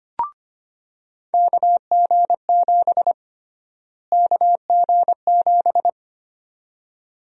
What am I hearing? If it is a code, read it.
Morse code, heard as a keyed tone, about 25 wpm, 700 Hz. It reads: KG7 KG7
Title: USA Call Sign Prefixes - Repeated 2x - 25wpm